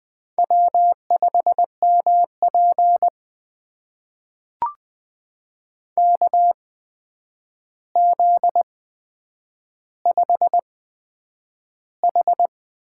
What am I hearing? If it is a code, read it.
Morse code, heard as a keyed tone, about 20 wpm, spaced as for 9 wpm, 700 Hz. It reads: W5MP KZ5H
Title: USA Call Signs - 4 Characters - 8x Character Spacing - 20wpm